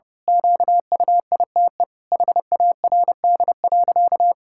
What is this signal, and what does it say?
Morse code, heard as a keyed tone, about 30 wpm, 700 Hz. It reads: QUITE HARD.
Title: Pangram Challenge v2 - 30wpm